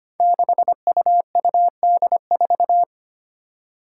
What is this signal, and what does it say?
Morse code, heard as a keyed tone, about 25 wpm, 700 Hz. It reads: 6UUD4
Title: Five Challenging Letter-Number Combinations 25wpm